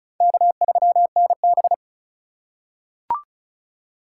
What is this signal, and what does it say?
Morse code, heard as a keyed tone, about 34 wpm, 700 Hz. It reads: K3NB
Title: USA Call Signs - 4 Characters - 3x Character Spacing - 35wpm